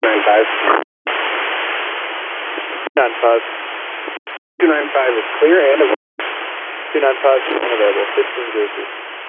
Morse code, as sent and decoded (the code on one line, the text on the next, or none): none